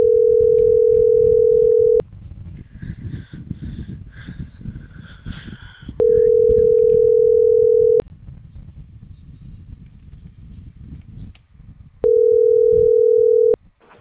Ringback tone (North America), 440 + 480 Hz.